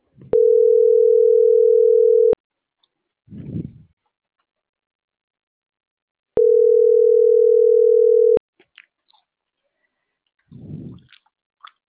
Ringback tone (North America), 440 + 480 Hz.